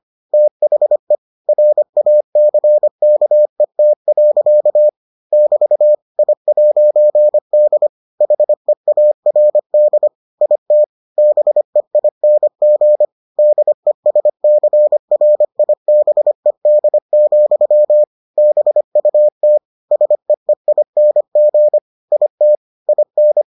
Morse code, heard as a keyed tone, about 25 wpm, 600 Hz. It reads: THE RACKET. <BT> I'D HEARD IT BEING DESCRIBED, BUT SEEING IT IN